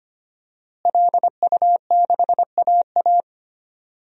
Morse code, heard as a keyed tone, about 25 wpm, 700 Hz. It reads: LU6AA
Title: DXCC-100 Call Signs - 5 Characters - Repeated 3x - 25wpm